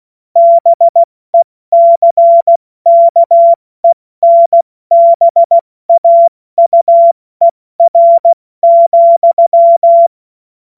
Morse code, heard as a keyed tone, about 16 wpm, 680 Hz. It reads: BECKENBAUER,
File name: Morsekurs / Witze / Witz 02. Februar 2021, 70 F80